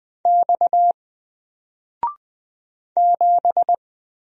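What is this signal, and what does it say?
Morse code, heard as a keyed tone, about 20 wpm, 700 Hz. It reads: X 7
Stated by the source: Single letter and number Morse drill at 20 wpm, from N4LCD.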